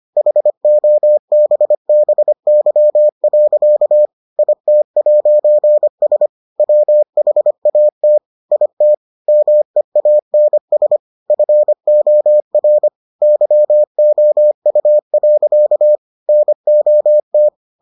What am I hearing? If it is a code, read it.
Morse code, heard as a keyed tone, about 25 wpm, 600 Hz. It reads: HOBBY. IT'S WHAT IT MEANS FOR YOU. NOT